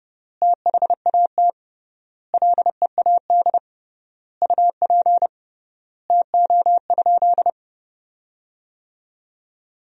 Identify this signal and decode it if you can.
Morse code, heard as a keyed tone, about 32 wpm, 700 Hz. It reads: THAT LEAD UP TO?
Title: 3x Word-Spacing - Sentences from Top 500 Words - 30wpm